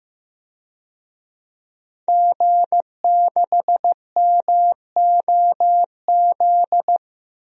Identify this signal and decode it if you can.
Morse code, heard as a keyed tone, about 15 wpm, 700 Hz. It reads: G6MOZ
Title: DXCC-100 Call Signs - 5 Characters - 15wpm